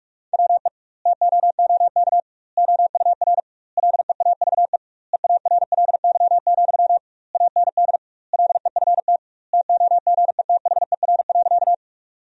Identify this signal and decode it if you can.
Morse code, heard as a keyed tone, about 46 wpm, 700 Hz. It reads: WE TOOK OUR LEAVE EARLY, AND LEFT TOGETHER.
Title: 2x Word-Spacing - Sentences from Top 400 Words - 45wpm